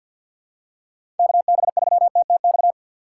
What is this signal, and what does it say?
Morse code, heard as a keyed tone, about 47 wpm, 700 Hz. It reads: KB3TTX
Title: USA Call Signs - 6 Characters - 3x Character Spacing - 50wpm